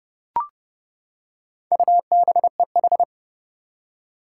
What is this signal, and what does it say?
Morse code, heard as a keyed tone, about 32 wpm, 700 Hz. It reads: UBEH
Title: Four Challenging Letter-Number Combinations 30wpm